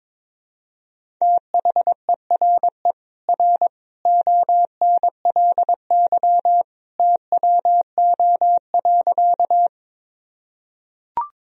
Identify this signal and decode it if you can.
Morse code, heard as a keyed tone, about 22 wpm, 700 Hz. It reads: THERE R ONLY TWO.
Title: Sentences from Top 200 Words and Abbreviations 22wpm